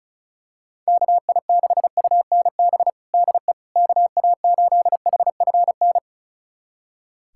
Morse code, heard as a keyed tone, about 36 wpm, 700 Hz. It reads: KI6UNB DE KA8HFN
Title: USA Callsign DE Callsign - 6 Characters - 35wpm